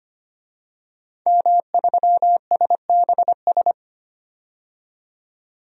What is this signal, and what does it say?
Morse code, heard as a keyed tone, about 25 wpm, 700 Hz. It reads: M3SBS
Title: DXCC-100 Call Signs - 5 Characters - 25wpm